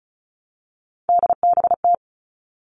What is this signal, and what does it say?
Morse code, heard as a keyed tone, about 34 wpm, 700 Hz. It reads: DBT
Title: All Cut Numbers - 3-Digit 35wpm